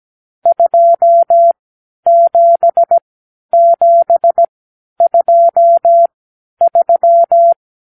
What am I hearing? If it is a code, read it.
Morse code, heard as a keyed tone, about 17 wpm, spaced as for 10 wpm, 680 Hz. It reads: 27723
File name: k7qo / Lesson 52 (8)